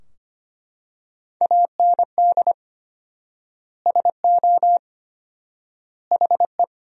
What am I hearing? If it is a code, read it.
Morse code, heard as a keyed tone, about 25 wpm, 700 Hz. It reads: AND SO HE
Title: 4x Word-Spacing - Sentences from Top 500 Words - 25wpm